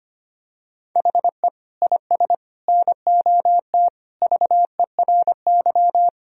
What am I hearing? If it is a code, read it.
Morse code, heard as a keyed tone, about 25 wpm, 700 Hz. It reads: HE IS NOT VERY
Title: Sentences from Top 1000 Words - Review Part 1 - 25wpm